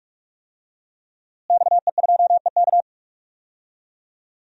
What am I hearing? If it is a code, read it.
Morse code, heard as a keyed tone, about 42 wpm, 700 Hz. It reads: XE2EK
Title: DXCC-100 Call Signs - 5 Characters - Repeated 3x - 45wpm